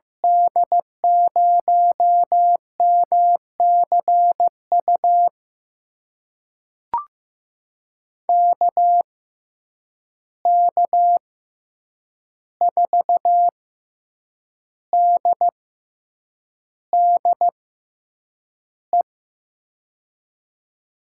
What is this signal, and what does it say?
Morse code, heard as a keyed tone, about 15 wpm, spaced as for 7 wpm, 700 Hz. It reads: D0MCU KK4DDE
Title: USA Call Signs - 6 Characters - 6x Character Spacing - 15wpm